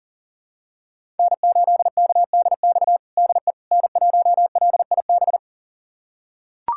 Morse code, heard as a keyed tone, about 40 wpm, 700 Hz. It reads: N8KDX DE N1LIB
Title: USA Callsign DE Callsign - 5 Characters - 40wpm